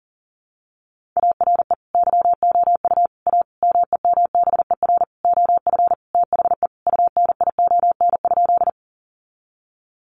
Morse code, heard as a keyed tone, about 40 wpm, 700 Hz. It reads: ARE YOU A MEMBER OF THE UNION?
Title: Sentences from Top 1400 Words 40wpm